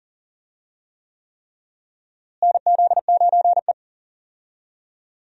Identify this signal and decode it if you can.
Morse code, heard as a keyed tone, about 40 wpm, 700 Hz. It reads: NZ9E
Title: USA Call Signs - 4 Characters - Repeated 3x - 40wpm